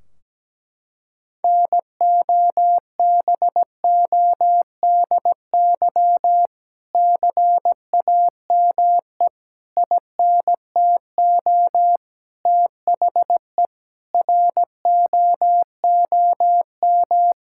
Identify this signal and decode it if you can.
Morse code, heard as a keyed tone, about 17 wpm, 700 Hz. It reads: NOBODY CAME INTO THE ROOM
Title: Sentences from Top 1600 Words 17wpm